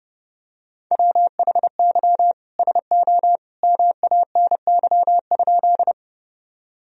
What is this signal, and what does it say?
Morse code, heard as a keyed tone, about 30 wpm, 700 Hz. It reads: WHY SO MANY?
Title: Sentences from Top 1000 Words - Review Part 1 - 30wpm